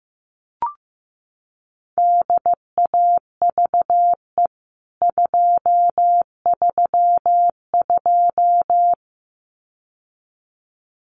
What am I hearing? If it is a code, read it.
Morse code, heard as a keyed tone, about 15 wpm, 700 Hz. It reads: DAVE 232
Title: CWT Exchange 15wpm